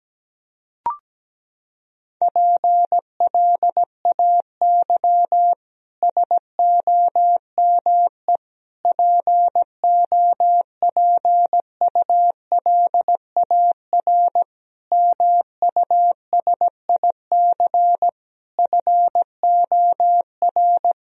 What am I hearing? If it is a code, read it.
Morse code, heard as a keyed tone, about 17 wpm, 700 Hz. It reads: PLAY SOME POPULAR MUSIC FOR
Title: Sentences from Top 1000 Words - Review Part 3 - 17wpm